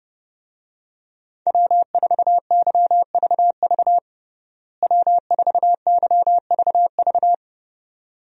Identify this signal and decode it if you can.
Morse code, heard as a keyed tone, about 30 wpm, 700 Hz. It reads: W4YVV W4YVV
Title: USA Call Signs - 5 Characters - Repeated 2x - 30wpm